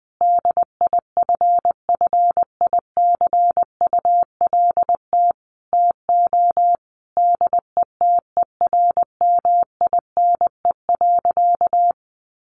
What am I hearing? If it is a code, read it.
Morse code, heard as a keyed tone, about 19 wpm, 700 Hz. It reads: DIFFICULT TO DETERMINE.
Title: Sentences from Top 1500 Words 20wpm